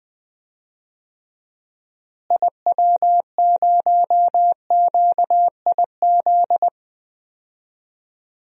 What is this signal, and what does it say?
Morse code, heard as a keyed tone, about 21 wpm, 700 Hz. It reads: IW0QIZ
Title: DXCC-100 Call Signs - 6 Characters - Repeated 3x - 20wpm